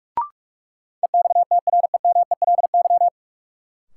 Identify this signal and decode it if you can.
Morse code, heard as a keyed tone, about 46 wpm, 700 Hz. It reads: EXTREMELY
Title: Sentences from Top 1200 Words - New Word Drill Alt - 45wpm